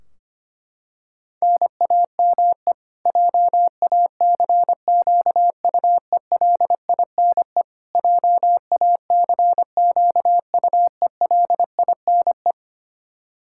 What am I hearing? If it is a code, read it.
Morse code, heard as a keyed tone, about 25 wpm, 700 Hz. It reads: NAME JACQUELINE JACQUELINE
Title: K1USN SST QSO Practice 25wpm